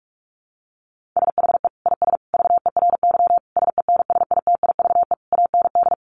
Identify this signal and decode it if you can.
Morse code, heard as a keyed tone, about 44 wpm, 700 Hz. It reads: SHE IS VERY SENSITIVE AND
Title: Sentences from Top 1400 Words 45wpm